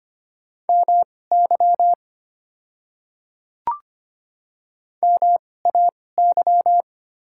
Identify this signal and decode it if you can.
Morse code, heard as a keyed tone, about 25 wpm, spaced as for 18 wpm, 700 Hz. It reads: MY MAY
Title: 2x Character Spacing - Top 100 Words - 25wpm